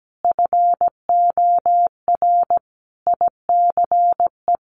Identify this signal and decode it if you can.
Morse code, heard as a keyed tone, about 17 wpm, 700 Hz. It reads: FOR ICE
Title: Sentences from Top 1500 Words 17wpm